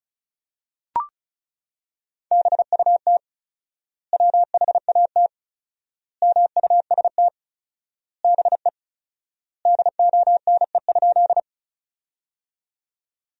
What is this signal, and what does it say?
Morse code, heard as a keyed tone, about 34 wpm, 700 Hz. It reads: BUT WHAT MUST BE DONE?
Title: 4x Word-Spacing - Sentences from Top 500 Words - 35wpm